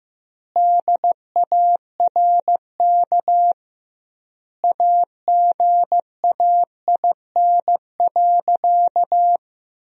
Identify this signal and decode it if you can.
Morse code, heard as a keyed tone, about 15 wpm, 700 Hz. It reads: DARK AGAIN.
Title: 2x Word-Spacing - Sentences from Top 500 Words - 15wpm